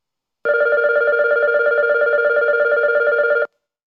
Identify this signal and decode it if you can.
There is no code.